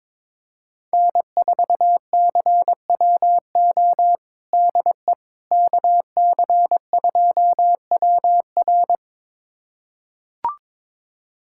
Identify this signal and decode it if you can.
Morse code, heard as a keyed tone, about 22 wpm, 700 Hz. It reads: N4CWO DE KC2WR
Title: USA Callsign DE Callsign - 5 Characters - 22wpm